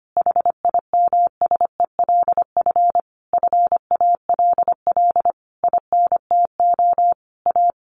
Morse code, heard as a keyed tone, about 25 wpm, 700 Hz. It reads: HIMSELF FALL INTO A
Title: A Sound Of Thunder 25wpm